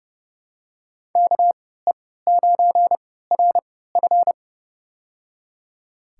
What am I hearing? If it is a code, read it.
Morse code, heard as a keyed tone, about 30 wpm, spaced as for 17 wpm, 700 Hz. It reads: KE9RF
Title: USA Call Signs - 5 Characters - 3x Character Spacing - 30wpm